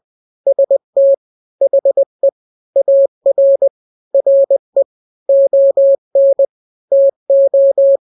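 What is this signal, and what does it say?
Morse code, heard as a keyed tone, about 20 wpm, 550 Hz. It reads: ST HE AR RE ON TO
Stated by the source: Morse drill of two-letter n-grams at 20/18 WPM (Farnsworth), from KK7RR.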